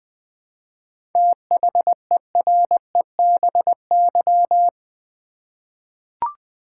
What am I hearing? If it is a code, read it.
Morse code, heard as a keyed tone, about 20 wpm, 700 Hz. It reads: THEREBY
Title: 7 Letter Words - First 3 Letters Given - 20wpm